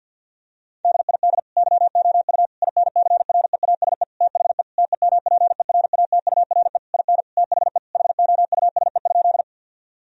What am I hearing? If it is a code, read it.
Morse code, heard as a keyed tone, about 50 wpm, 700 Hz. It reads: DID YOU INCREASE THE TEMPERATURE IN THE HOUSE?
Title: Sentences from Top 1000 Words - Review Part 4 - 50wpm